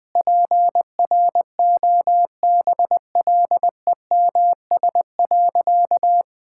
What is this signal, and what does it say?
Morse code, heard as a keyed tone, about 20 wpm, 700 Hz. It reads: PROBLEMS.
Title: Easy Sentences with Hints - 20wpm